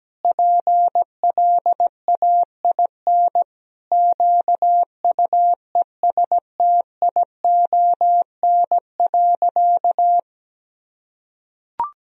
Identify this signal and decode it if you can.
Morse code, heard as a keyed tone, about 17 wpm, 700 Hz. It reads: PLAIN QUESTION.
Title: Sentences from Top 1000 Words - Review Part 2 - 17wpm